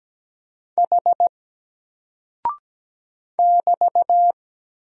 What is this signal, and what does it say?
Morse code, heard as a keyed tone, about 17 wpm, 700 Hz. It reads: H <BT>
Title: Single Character (Letter:Number:Punctuation:Prosign) Alt 17wpm